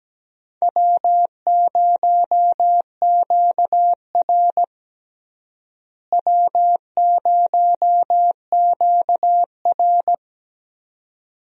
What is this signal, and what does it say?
Morse code, heard as a keyed tone, about 17 wpm, 700 Hz. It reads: W0QR W0QR
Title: USA Call Signs - 4 Characters - Repeated 2x - 17wpm